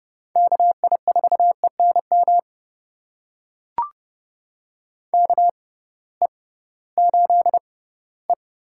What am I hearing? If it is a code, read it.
Morse code, heard as a keyed tone, about 30 wpm, spaced as for 17 wpm, 700 Hz. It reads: KI4ENM KE8E
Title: USA Call Signs - 6 Characters - 6x Character Spacing - 30wpm